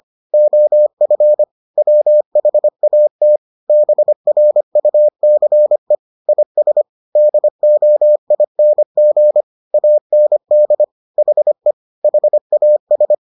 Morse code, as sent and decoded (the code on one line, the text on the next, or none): --- ..-. / .-- .... .- - / -... .-. ..- -.-. / .. ... / -.. --- .. -. --. / .- -. -.. / .... / .... .- ...
OF WHAT BRUCE IS DOING AND HE HAS